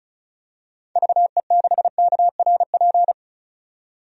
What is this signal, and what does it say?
Morse code, heard as a keyed tone, about 34 wpm, 700 Hz. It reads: VE6KRP
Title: DXCC-100 Call Signs - 6 Characters - 35wpm